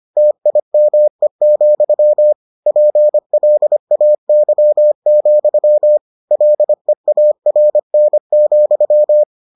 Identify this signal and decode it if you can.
Morse code, heard as a keyed tone, about 25 wpm, 600 Hz. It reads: TIME, PLAY, LEARN,